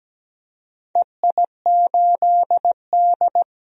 Morse code, heard as a keyed tone, about 17 wpm, 700 Hz. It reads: EI8D
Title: DXCC-100 Call Signs - 4 Characters - 17wpm